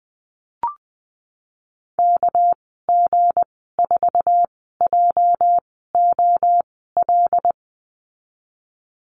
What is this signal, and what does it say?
Morse code, heard as a keyed tone, about 20 wpm, spaced as for 15 wpm, 700 Hz. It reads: KG4JOL
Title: 207 Lesson 33g - Letter j - Call Signs 20wpm